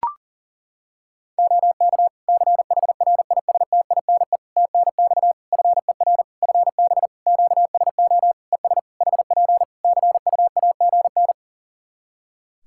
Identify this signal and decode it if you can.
Morse code, heard as a keyed tone, about 40 wpm, 700 Hz. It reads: OK CHRISTINE TNX FER FB QSO ES HP CUAGN
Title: Practice QSO - Exchange 4 - Ending - 40wpm